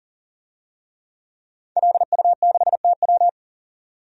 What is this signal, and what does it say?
Morse code, heard as a keyed tone, about 40 wpm, 700 Hz. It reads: LU6TW
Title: DXCC-100 Call Signs - 5 Characters - 40wpm